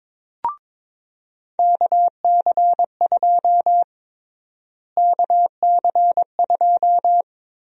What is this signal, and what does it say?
Morse code, heard as a keyed tone, about 22 wpm, 700 Hz. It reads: KC2 KC2
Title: USA Call Sign Prefixes - Repeated 2x - 22wpm